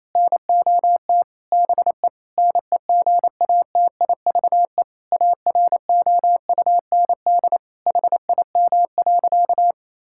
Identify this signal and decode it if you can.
Morse code, heard as a keyed tone, about 27 wpm, 700 Hz. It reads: NOT BE NEGATIVE AROUND HIM.
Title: Sentences from Top 1000 Words - Review Part 6 - 28wpm